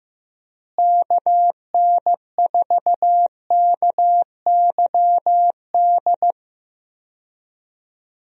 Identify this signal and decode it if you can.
Morse code, heard as a keyed tone, about 15 wpm, 700 Hz. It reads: KN4KYD
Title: USA Call Signs - 6 Characters - Repeated 2x - 15wpm